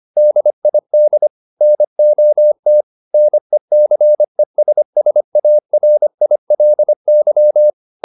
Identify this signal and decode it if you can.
Morse code, heard as a keyed tone, about 25 wpm, 600 Hz. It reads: DID NOT NECESSARILY
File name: the old man and the sea part 1 25wpm